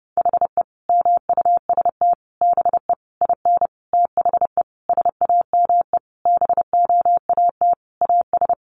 Morse code, heard as a keyed tone, about 30 wpm, 700 Hz. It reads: HE MUST BE IN THE SAME BOAT AS
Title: Sentences from Top 500 Words 30wpm